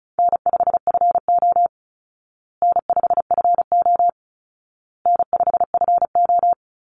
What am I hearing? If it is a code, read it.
Morse code, heard as a keyed tone, about 34 wpm, 700 Hz. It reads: N5FO N5FO N5FO
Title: Easy Call Signs Repeated 3x - 35wpm